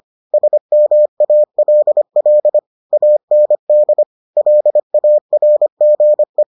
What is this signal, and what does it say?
Morse code, heard as a keyed tone, about 25 wpm, 600 Hz. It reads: SMALL AND LARGE